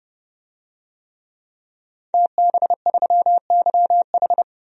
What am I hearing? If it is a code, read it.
Morse code, heard as a keyed tone, about 30 wpm, 700 Hz. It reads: TB3YH
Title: DXCC-100 Call Signs - 5 Characters - 30wpm